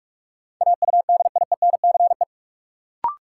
Morse code, heard as a keyed tone, about 46 wpm, 700 Hz. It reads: AUDIENCE
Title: Sentences from Top 800 Words - New Word Drill - 45wpm